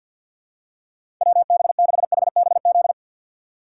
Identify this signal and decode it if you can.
Morse code, heard as a keyed tone, about 52 wpm, 700 Hz. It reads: WB6HBZ
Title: USA Call Signs - 6 Characters - 4x Character Spacing - 50wpm